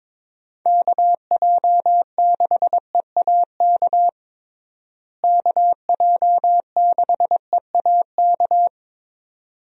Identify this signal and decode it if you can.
Morse code, heard as a keyed tone, about 22 wpm, 700 Hz. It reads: KJ6EAK KJ6EAK
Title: USA Call Signs - 6 Characters - Repeated 2x - 22wpm